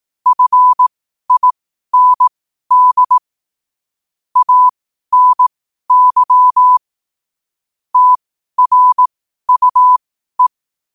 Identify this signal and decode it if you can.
Morse code, heard as a keyed tone, about 18 wpm, spaced as for 12 wpm, 1000 Hz. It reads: FIND ANY TRUE
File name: qotdfiles / 12wpm